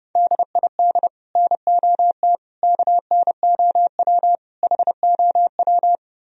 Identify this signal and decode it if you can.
Morse code, heard as a keyed tone, about 34 wpm, 700 Hz. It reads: DID NOT KNOW HOW
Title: Speed-Racing - Sets of 5 Words 30wpm